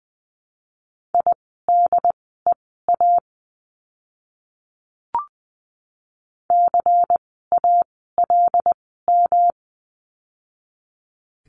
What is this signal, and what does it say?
Morse code, heard as a keyed tone, about 20 wpm, spaced as for 14 wpm, 700 Hz. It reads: IDEA CALM
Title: 108 Lesson 18f - Letter m - Words 20wpm